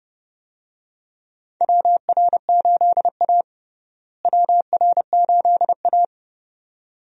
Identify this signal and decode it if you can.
Morse code, heard as a keyed tone, about 30 wpm, 700 Hz. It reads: WR8A WR8A
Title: USA Call Signs - 4 Characters - Repeated 2x - 30wpm